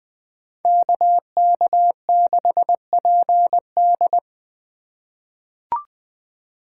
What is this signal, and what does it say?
Morse code, heard as a keyed tone, about 20 wpm, 700 Hz. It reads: KK6PD
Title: USA Call Signs - 5 Characters - 8x Character Spacing - 20wpm